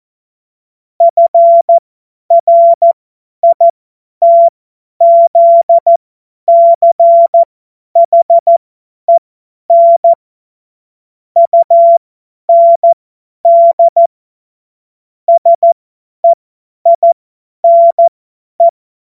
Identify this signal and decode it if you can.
Morse code, heard as a keyed tone, about 14 wpm, spaced as for 10 wpm, 680 Hz. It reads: FRITZCHEN UND SEINE
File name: Morsekurs / Witze / Witz 12. Oktober 2021, 50 F70